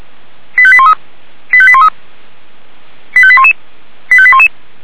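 Five-tone selective call (ZVEI), heard as quick runs of five tones, tones of about 80 ms. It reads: 85813 85813 85811 85811
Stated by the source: Received on an SDR; ZVEI five-tone selective call.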